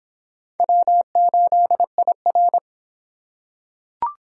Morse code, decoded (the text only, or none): W8IR